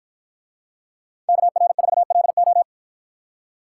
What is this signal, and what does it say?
Morse code, heard as a keyed tone, about 50 wpm, 700 Hz. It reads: KR4LO